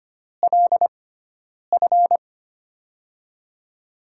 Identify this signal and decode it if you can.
Morse code, heard as a keyed tone, about 25 wpm, spaced as for 9 wpm, 700 Hz. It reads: LF
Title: USA Call Signs - 5 Characters - 6x Character Spacing - 25wpm